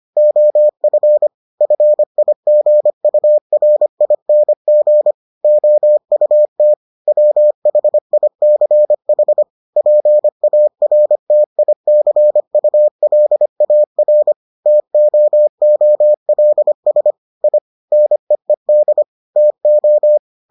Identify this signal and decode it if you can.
Morse code, heard as a keyed tone, about 25 wpm, 600 Hz. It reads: OF FIGURING OUT WHICH PARTICULAR TOOLS I NEED TO